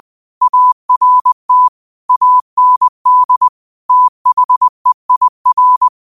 Morse code, heard as a keyed tone, about 20 wpm, 1000 Hz. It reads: ART AND THEIR